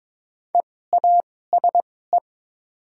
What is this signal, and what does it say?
Morse code, heard as a keyed tone, about 22 wpm, spaced as for 16 wpm, 700 Hz. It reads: EASE
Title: 2x Character Spacing - Top 500 Words - 22wpm